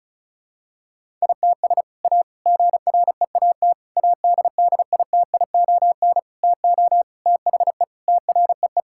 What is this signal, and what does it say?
Morse code, heard as a keyed tone, about 34 wpm, 700 Hz. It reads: ITS A GREAT ADDITION TO THE TREE
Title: Sentences from Top 1000 Words - Review Part 6 - 35wpm